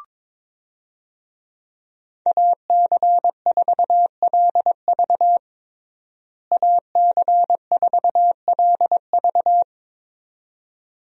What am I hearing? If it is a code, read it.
Morse code, heard as a keyed tone, about 22 wpm, 700 Hz. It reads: AC4LV AC4LV
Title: USA Call Signs - 5 Characters - Repeated 2x - 22wpm